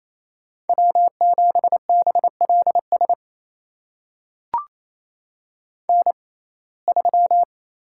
Morse code, heard as a keyed tone, about 28 wpm, spaced as for 20 wpm, 700 Hz. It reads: W7BLS N3
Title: USA Call Signs - 5 Characters - 6x Character Spacing - 28wpm